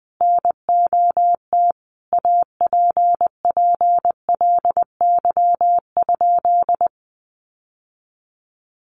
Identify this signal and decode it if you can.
Morse code, heard as a keyed tone, about 20 wpm, 700 Hz. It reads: NOT APPLY?
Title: Sentences from Top 1200 Words 20wpm